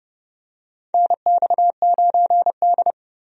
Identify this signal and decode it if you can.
Morse code, heard as a keyed tone, about 32 wpm, 700 Hz. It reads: NX9D
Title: USA Call Signs - 4 Characters - Repeated 2x - 30wpm